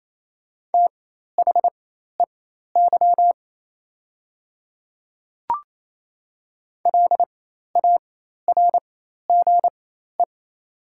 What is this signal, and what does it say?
Morse code, heard as a keyed tone, about 28 wpm, spaced as for 13 wpm, 700 Hz. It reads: THEY LARGE
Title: 4x Character Spacing - Top 200 Words - 28wpm